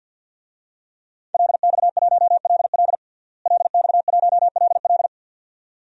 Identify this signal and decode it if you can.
Morse code, heard as a keyed tone, about 53 wpm, 700 Hz. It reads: LX1LL LX1LL
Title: DXCC-100 Call Signs - 5 Characters - Repeated 2x - 50wpm